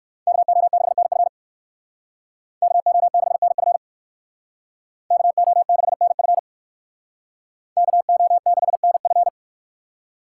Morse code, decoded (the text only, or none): KO6NF KO6NF KO6NF KO6NF